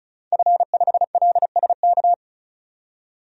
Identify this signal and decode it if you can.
Morse code, heard as a keyed tone, about 35 wpm, 700 Hz. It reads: F5LSK
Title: DXCC-100 Call Signs - 5 Characters - 35wpm